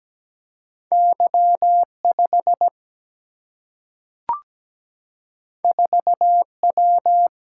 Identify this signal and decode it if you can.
Morse code, heard as a keyed tone, about 17 wpm, 700 Hz. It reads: Y5 4W
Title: Single-Letter Single-Number 17wpm